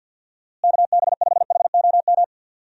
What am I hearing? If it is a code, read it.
Morse code, heard as a keyed tone, about 53 wpm, 700 Hz. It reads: KB5HOM
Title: USA Call Signs - 6 Characters - Repeated 3x - 50wpm